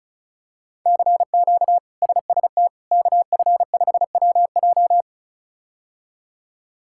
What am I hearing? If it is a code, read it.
Morse code, heard as a keyed tone, about 36 wpm, 700 Hz. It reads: CQ SST KF5WJ
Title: K1USN SST QSO Practice - 35wpm